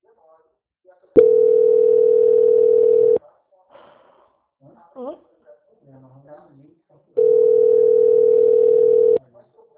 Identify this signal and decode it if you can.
Ringback tone (North America), 440 + 480 Hz, with speech.
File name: out- -2035- - - .66